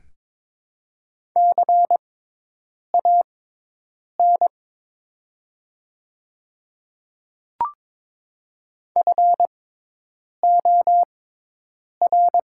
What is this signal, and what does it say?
Morse code, heard as a keyed tone, about 22 wpm, spaced as for 8 wpm, 700 Hz. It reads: CAN FOR
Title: 6x Character Spacing - Top 200 Words - 22wpm